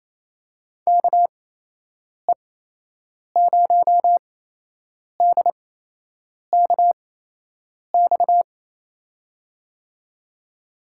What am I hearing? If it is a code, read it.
Morse code, heard as a keyed tone, about 28 wpm, spaced as for 8 wpm, 700 Hz. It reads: KE0DKX
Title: USA Call Signs - 6 Characters - 8x Character Spacing - 28wpm